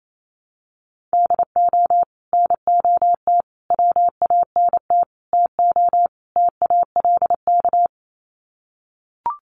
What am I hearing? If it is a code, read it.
Morse code, heard as a keyed tone, about 28 wpm, 700 Hz. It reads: DO NOT WANT TO TALK
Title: Sets of 5 Words 28wpm